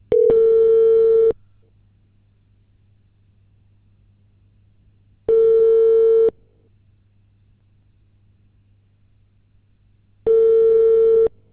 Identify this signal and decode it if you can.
Ringback tone (China, 450 Hz).